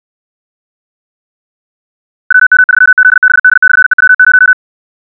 Morse code, heard as a keyed tone, about 64 wpm, 1500 Hz. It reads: WM3ZBD/EW0
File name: ja7uhv / SampleHST3